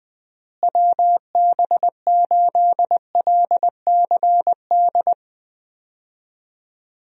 Morse code, heard as a keyed tone, about 21 wpm, 700 Hz. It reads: WB8LCD
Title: USA Call Signs - 6 Characters - Repeated 2x - 20wpm